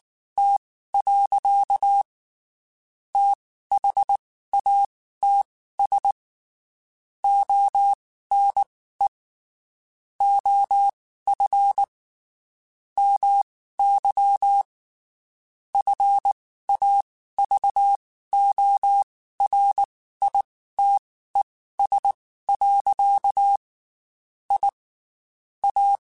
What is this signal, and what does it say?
Morse code, heard as a keyed tone, about 19 wpm, spaced as for 13 wpm, 800 Hz. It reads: T. THATS ONE OF MY FAVORITES. IA
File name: QSOs / QSOs-mp3s / WD0FBT 1@18